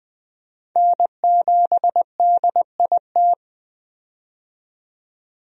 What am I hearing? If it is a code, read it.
Morse code, heard as a keyed tone, about 20 wpm, 700 Hz. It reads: N7DIT
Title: USA Call Signs - 5 Characters - 20wpm